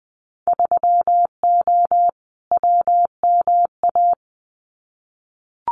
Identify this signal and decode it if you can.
Morse code, heard as a keyed tone, about 20 wpm, 700 Hz. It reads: 3O WMA
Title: Winter Field Day Exchanges Encore 20wpm